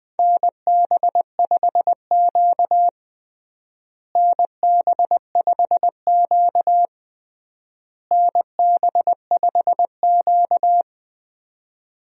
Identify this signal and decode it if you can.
Morse code, heard as a keyed tone, about 20 wpm, 700 Hz. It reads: NB5Q NB5Q NB5Q
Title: USA Call Signs - 4 Characters - Repeated 3x - 20wpm